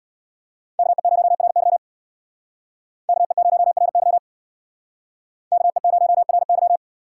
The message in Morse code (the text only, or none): KE0RY KE0RY KE0RY